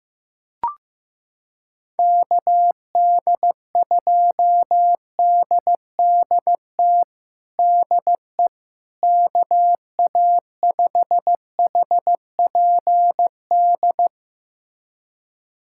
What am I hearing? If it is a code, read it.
Morse code, heard as a keyed tone, about 16 wpm, 700 Hz. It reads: KD2DDT DE KA5HPD
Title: USA Callsign DE Callsign - 6 Characters - 15wpm